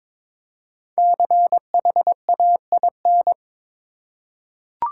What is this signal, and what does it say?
Morse code, heard as a keyed tone, about 22 wpm, 700 Hz. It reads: CHAIN
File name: 5 Letter Words - First 3 Letters Given - 22wpm